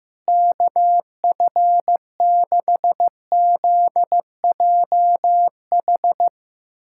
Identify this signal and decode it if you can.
Morse code, heard as a keyed tone, about 15 wpm, 700 Hz. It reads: KF6ZJH
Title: USA Callsign DE Callsign 15wpm